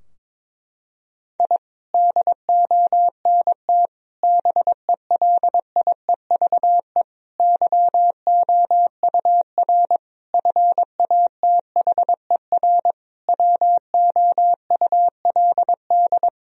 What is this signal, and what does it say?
Morse code, heard as a keyed tone, about 22 wpm, 700 Hz. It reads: I DONT BELIEVE YOUR FATHER WOULD
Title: Sentences from Top 1000 Words - Review Part 3 - 22wpm